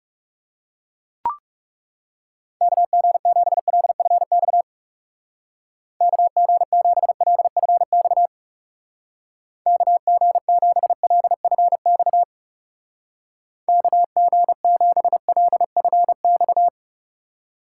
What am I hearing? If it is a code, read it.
Morse code, heard as a keyed tone, about 38 wpm, 700 Hz. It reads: KG7LFX KG7LFX KG7LFX KG7LFX